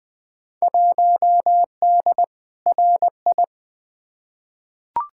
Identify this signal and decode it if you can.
Morse code, heard as a keyed tone, about 20 wpm, 700 Hz. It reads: 1D RI
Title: Field Day Exchanges - 20wpm